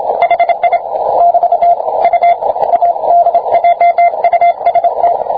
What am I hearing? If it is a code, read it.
Morse code, heard as a keyed tone, about 27 wpm, 690 Hz, with radio noise. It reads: HI <BT> U ADJUST